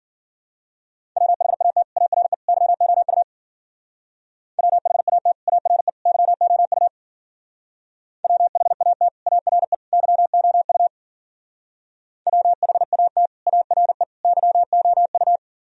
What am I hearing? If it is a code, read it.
Morse code, heard as a keyed tone, about 50 wpm, 700 Hz. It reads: WHAT ARE YOU WHAT ARE YOU WHAT ARE YOU WHAT ARE YOU
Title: Speed-Racing - Sets of 3 Words 40wpm